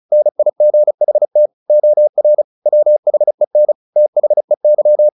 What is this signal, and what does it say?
Morse code, heard as a keyed tone, about 34 wpm, 600 Hz. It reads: NIGHT OR WHEN THEY
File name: the old man and the sea / the old man and the sea 1 35wpm